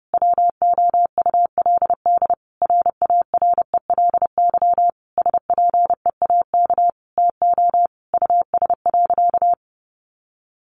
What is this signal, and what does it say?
Morse code, heard as a keyed tone, about 30 wpm, 700 Hz. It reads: WOULD RARELY SPEAK TO US.
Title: Sentences from Top 1400 Words 30wpm